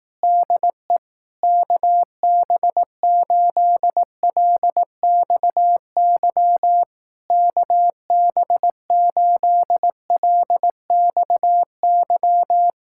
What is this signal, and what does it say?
Morse code, heard as a keyed tone, about 18 wpm, 700 Hz. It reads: DE KB8LXY KB8LXY
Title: Speed-Racing Common QSO Phrases Encore - 15wpm